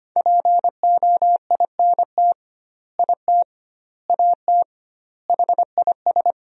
Morse code, heard as a keyed tone, about 25 wpm, 700 Hz. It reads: POINT IT AT HIS
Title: 2x Word-Spacing - Sentences from Top 300 Words - 25wpm